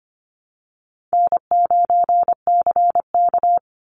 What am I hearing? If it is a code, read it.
Morse code, heard as a keyed tone, about 25 wpm, 700 Hz. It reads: N9CK
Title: Call Signs Repeated 3x - 25wpm